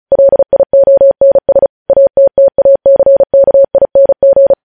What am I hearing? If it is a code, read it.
Morse code, heard as a keyed tone, about 36 wpm, 550 Hz. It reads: LIONS ATTACKING
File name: podcast / News headlines-1x-35x35